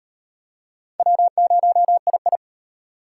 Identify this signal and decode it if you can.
Morse code, heard as a keyed tone, about 38 wpm, 700 Hz. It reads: W0II